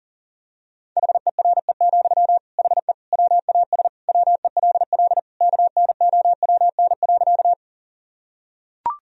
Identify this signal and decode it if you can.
Morse code, heard as a keyed tone, about 40 wpm, 700 Hz. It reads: HERE, HE WAS WELL KNOWN.
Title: Sentences from Top 1000 Words - Review Part 3 - 40wpm